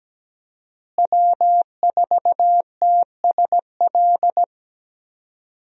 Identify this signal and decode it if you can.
Morse code, heard as a keyed tone, about 17 wpm, 700 Hz. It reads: W4TSL
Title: USA Call Signs - 5 Characters - Repeated 2x - 17wpm